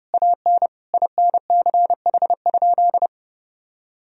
Morse code, heard as a keyed tone, about 30 wpm, 700 Hz. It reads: AN INCH?
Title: Sentences from Top 1000 Words - Review Part 3 - 30wpm